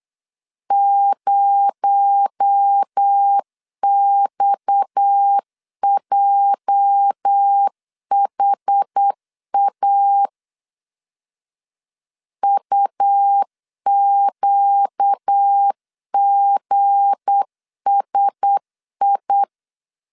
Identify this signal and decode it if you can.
Morse code, heard as a keyed tone, about 8 wpm, 800 Hz. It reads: 0XJHA UQGSI
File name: LEZIONI / 8 YJ45 / 8 YJ45 8wpm 3214